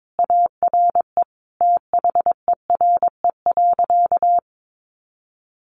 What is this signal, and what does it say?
Morse code, heard as a keyed tone, about 22 wpm, 700 Hz. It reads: ARE THERE.
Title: Sentences from Top 500 Words 22wpm